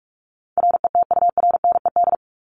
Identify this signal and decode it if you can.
Morse code, heard as a keyed tone, about 44 wpm, 700 Hz. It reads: RETURNED
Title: ED Suffix - Top 100 Words with Base Words 45wpm